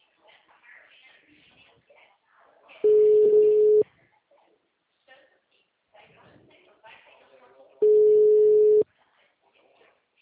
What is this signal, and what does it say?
Ringback tone (Europe, 425 Hz), with speech.
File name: out- -2017- - - .99410